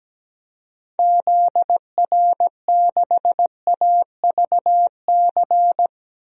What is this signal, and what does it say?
Morse code, heard as a keyed tone, about 17 wpm, 700 Hz. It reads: ZR6AVC
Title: DXCC-100 Call Signs - 6 Characters - 17wpm